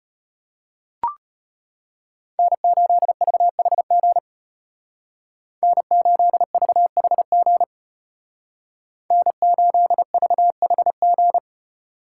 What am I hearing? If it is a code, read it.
Morse code, heard as a keyed tone, about 30 wpm, 700 Hz. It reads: N8VHG N8VHG N8VHG
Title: Speed Racing - USA 1x3 Call Signs 25wpm